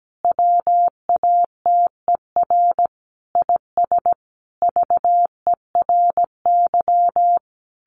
Morse code, heard as a keyed tone, about 17 wpm, 700 Hz. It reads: WATER IS VERY
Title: Sentences from Top 1500 Words 17wpm